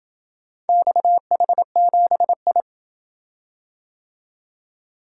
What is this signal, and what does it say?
Morse code, heard as a keyed tone, about 27 wpm, 700 Hz. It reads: XH7I